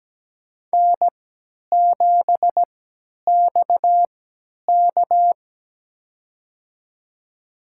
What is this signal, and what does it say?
Morse code, heard as a keyed tone, about 17 wpm, spaced as for 10 wpm, 700 Hz. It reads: N7XK
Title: USA Call Signs - 3x Character Spacing - 17wpm